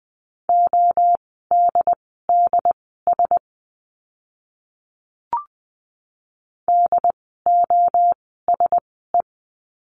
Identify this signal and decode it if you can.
Morse code, heard as a keyed tone, about 20 wpm, spaced as for 14 wpm, 700 Hz. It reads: ODDS DOSE
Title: 068 Lesson 12f - Letter d - Words 20wpm